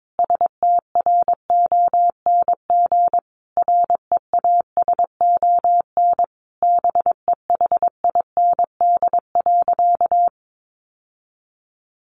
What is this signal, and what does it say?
Morse code, heard as a keyed tone, about 22 wpm, 700 Hz. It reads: STRONG REASON BEHIND.
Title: Sentences from Top 700 Words 22wpm